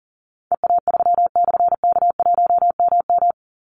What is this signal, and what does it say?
Morse code, heard as a keyed tone, about 40 wpm, 700 Hz. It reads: EA3/K1MM
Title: Call Signs - 40wpm